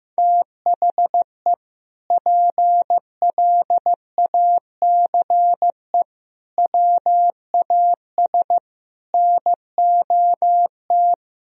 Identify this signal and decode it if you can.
Morse code, heard as a keyed tone, about 15 wpm, 700 Hz. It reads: THE PLACE WAS NOT
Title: Sentences from Top 1000 Words - Review Part 1 - 15wpm